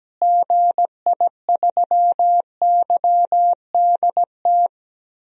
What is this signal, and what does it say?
Morse code, heard as a keyed tone, about 17 wpm, 700 Hz. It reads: GI3YDT
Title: DXCC-100 Call Signs - 6 Characters - Repeated 2x - 17wpm